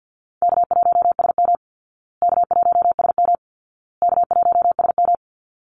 Morse code, heard as a keyed tone, about 50 wpm, 700 Hz. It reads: K1SM K1SM K1SM
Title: Easy Call Signs Repeated 3x - 50wpm